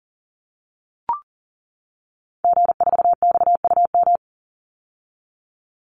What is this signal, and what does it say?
Morse code, heard as a keyed tone, about 40 wpm, 700 Hz. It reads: G4XUM
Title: Call Signs - 40wpm